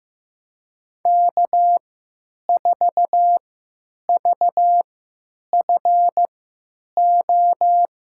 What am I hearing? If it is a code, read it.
Morse code, heard as a keyed tone, about 15 wpm, spaced as for 9 wpm, 700 Hz. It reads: K4VFO
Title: USA Call Signs - 5 Characters - 3x Character Spacing - 15wpm